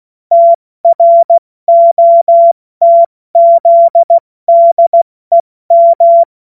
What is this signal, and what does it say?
Morse code, heard as a keyed tone, about 16 wpm, 680 Hz. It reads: TROTZDEM
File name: Morsekurs / Witze / Witz 31. Mai 2022 70 F80